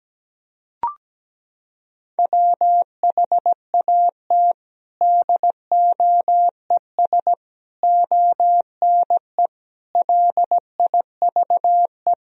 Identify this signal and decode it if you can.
Morse code, heard as a keyed tone, about 17 wpm, 700 Hz. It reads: WHAT DOES ONE LIVE
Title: Sentences from Top 200 Words and Abbreviations 17wpm